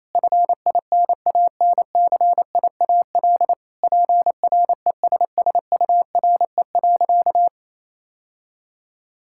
Morse code, heard as a keyed tone, about 28 wpm, 700 Hz. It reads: FINANCIAL PRESSURE.
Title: Sentences from Top 1000 Words - Review Part 6 - 28wpm